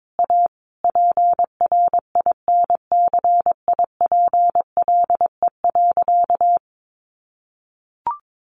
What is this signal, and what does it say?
Morse code, heard as a keyed tone, about 22 wpm, 700 Hz. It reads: A PRINCIPLE.
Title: Sentences from Top 1400 Words 22wpm